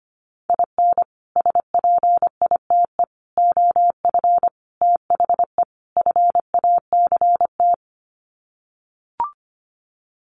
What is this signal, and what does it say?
Morse code, heard as a keyed tone, about 25 wpm, 700 Hz. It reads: IN SPITE OF THE FACT
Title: Sets of 5 Words Encore2 25wpm